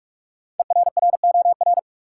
Morse code, heard as a keyed tone, about 43 wpm, 700 Hz. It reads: ERROR
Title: Five Letter Words - 45wpm